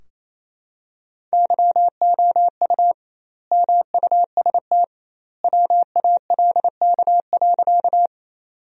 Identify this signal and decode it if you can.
Morse code, heard as a keyed tone, about 28 wpm, 700 Hz. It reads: YOU MUST WALK.
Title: 2x Word-Spacing - Sentences from Top 300 Words - 28wpm